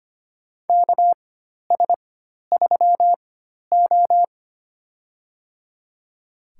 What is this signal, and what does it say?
Morse code, heard as a keyed tone, about 25 wpm, spaced as for 12 wpm, 690 Hz. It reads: KS3O
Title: USA Call Signs - 4x Character Spacing - 25wpm